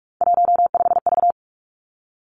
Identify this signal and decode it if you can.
Morse code, heard as a keyed tone, about 46 wpm, 700 Hz. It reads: 15V
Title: Three Challenging Letter-Number Combinations 45wpm